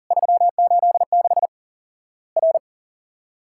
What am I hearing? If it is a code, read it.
Morse code, heard as a keyed tone, about 40 wpm, 700 Hz. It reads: 386 R
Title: Worked All Europe DX Contest - QTC Exchange - 40wpm